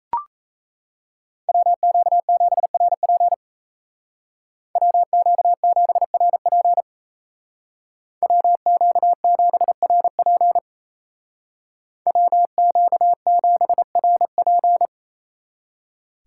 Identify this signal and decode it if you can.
Morse code, heard as a keyed tone, about 35 wpm, 700 Hz. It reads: WQ7RP WQ7RP WQ7RP WQ7RP
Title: Speed Racing - USA 2x2 Call Signs 28wpm